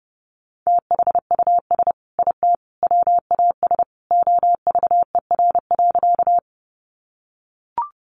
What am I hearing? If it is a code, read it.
Morse code, heard as a keyed tone, about 30 wpm, 700 Hz. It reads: THUS IT WAS OVER.
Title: Sentences from Top 800 Words 30wpm